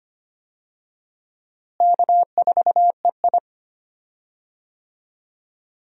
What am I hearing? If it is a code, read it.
Morse code, heard as a keyed tone, about 25 wpm, 700 Hz. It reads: K4EI
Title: USA Call Signs - 4 Characters - Repeated 3x - 25wpm